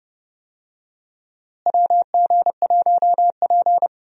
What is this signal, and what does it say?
Morse code, heard as a keyed tone, about 30 wpm, 700 Hz. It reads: WG1P
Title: USA Call Signs - 4 Characters - Repeated 2x - 30wpm